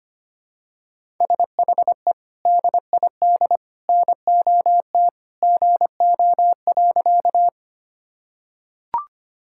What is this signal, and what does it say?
Morse code, heard as a keyed tone, about 25 wpm, 700 Hz. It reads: SHE DID NOT GO.
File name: practice / 25wpm / Sentences with 3-Word Phrases - 25wpm